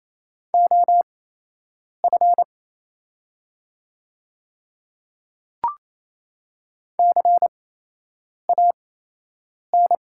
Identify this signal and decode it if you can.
Morse code, heard as a keyed tone, about 27 wpm, spaced as for 8 wpm, 700 Hz. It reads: OF CAN
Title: 8x Character Spacing - Top 100 Words - 28wpm